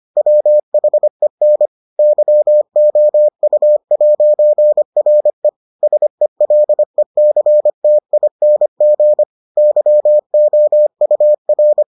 Morse code, heard as a keyed tone, about 25 wpm, 600 Hz. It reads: WHEN YOU'RE SELECTING YOUR